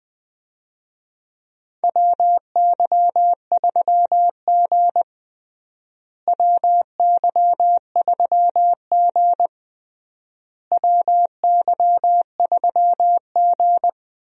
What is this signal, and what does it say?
Morse code, heard as a keyed tone, about 20 wpm, 700 Hz. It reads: WY3G WY3G WY3G
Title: USA Call Signs - 4 Characters - Repeated 3x - 20wpm